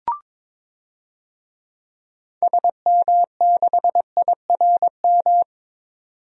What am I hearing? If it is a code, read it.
Morse code, heard as a keyed tone, about 21 wpm, 700 Hz. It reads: SM6IRM
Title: DXCC-100 Call Signs - 6 Characters - 22wpm